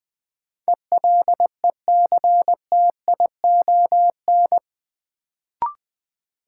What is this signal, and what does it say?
Morse code, heard as a keyed tone, about 20 wpm, 700 Hz. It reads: ELECTION
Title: Sentences from Top 1500 Words - New Word Drill Alt - 20wpm